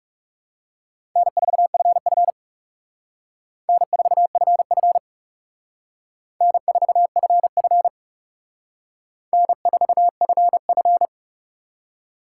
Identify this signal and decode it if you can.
Morse code, heard as a keyed tone, about 37 wpm, 700 Hz. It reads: N4FF N4FF N4FF N4FF
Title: Speed Racing - USA 1x2 Call Signs 30wpm